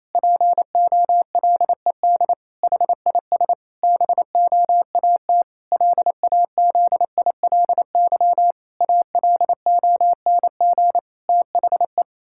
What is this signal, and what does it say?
Morse code, heard as a keyed tone, about 27 wpm, 700 Hz. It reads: POLED HIS BOAT LAZILY ALONG THE
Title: Pangram Challenge v2 - 28wpm